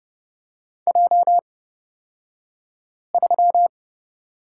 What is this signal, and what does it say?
Morse code, heard as a keyed tone, about 30 wpm, spaced as for 5 wpm, 700 Hz. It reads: J3
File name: Lesson 38d - Letter q - Two Characters 30wpm